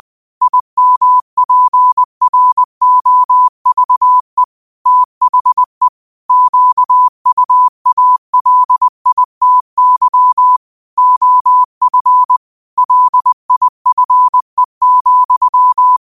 Morse code, heard as a keyed tone, about 20 wpm, 1000 Hz. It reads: IMPROVE THE QUALITY OF LIFE,
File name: qotdfiles / 20wpm